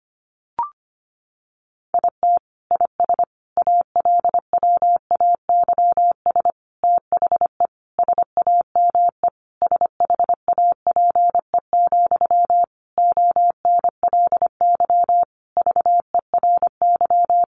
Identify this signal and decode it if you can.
Morse code, heard as a keyed tone, about 25 wpm, 700 Hz. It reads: IT IS ALWAYS THE SAME SHAPE, ONLY VERY
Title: Sentences from Top 1400 Words 25wpm